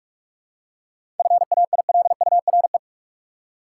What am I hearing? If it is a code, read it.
Morse code, heard as a keyed tone, about 43 wpm, 700 Hz. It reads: FAILURE
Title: Seven Letter Words - 45wpm